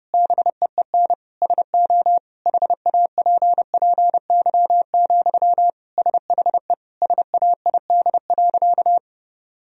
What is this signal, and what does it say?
Morse code, heard as a keyed tone, about 30 wpm, 700 Hz. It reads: BEEN SO HAPPY, SHE SAID.
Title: Sentences from Top 1000 Words - Review Part 1 - 30wpm